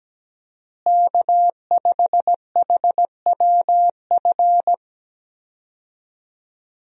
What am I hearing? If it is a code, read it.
Morse code, heard as a keyed tone, about 17 wpm, 700 Hz. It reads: K5HWF
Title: USA Call Signs - 5 Characters - Repeated 3x - 17wpm